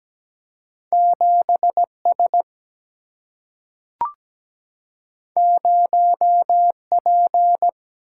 Morse code, heard as a keyed tone, about 17 wpm, 700 Hz. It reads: 7 S 0 P
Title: Single-Letter Single-Number 17wpm